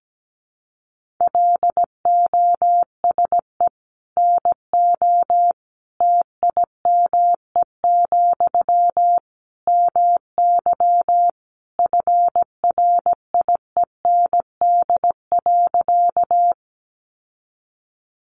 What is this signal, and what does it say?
Morse code, heard as a keyed tone, about 17 wpm, 700 Hz. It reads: LOSE NO TIME, MY FRIEND.
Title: Sentences from Top 1100 Words 17wpm